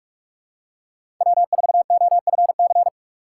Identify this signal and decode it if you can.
Morse code, heard as a keyed tone, about 42 wpm, 700 Hz. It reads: W4OFC
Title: USA Call Signs - 2x Character Spacing - 45wpm